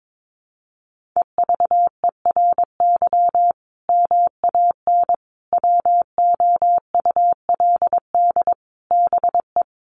Morse code, heard as a keyed tone, about 22 wpm, 700 Hz. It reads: EVERY MAN WOULD BE
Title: Sentences from Top 1500 Words 22wpm